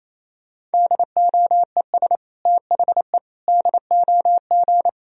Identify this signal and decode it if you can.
Morse code, heard as a keyed tone, about 27 wpm, 700 Hz. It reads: DOES THE DOG
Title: Sentences from Top 1000 Words - Review Part 3 - 28wpm